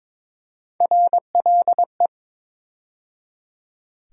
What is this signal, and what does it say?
Morse code, heard as a keyed tone, about 22 wpm, 700 Hz. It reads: RLE
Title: USA Call Sign Postfixes - 3 Letters 22wpm